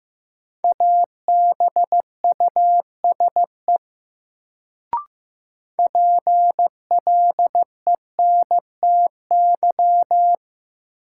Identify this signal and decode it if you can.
Morse code, heard as a keyed tone, about 15 wpm, 700 Hz. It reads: ABUSE PLENTY
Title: Sentences from Top 1300 Words - New Word Drill - 15wpm